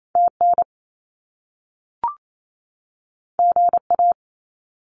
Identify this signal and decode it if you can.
Morse code, heard as a keyed tone, about 28 wpm, 700 Hz. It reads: TN GA
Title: US States 28wpm